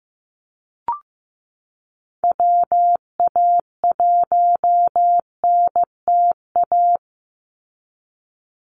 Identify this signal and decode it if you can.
Morse code, heard as a keyed tone, about 15 wpm, 700 Hz. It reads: WA1NTA
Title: Call Signs Repeated 2x - 15wpm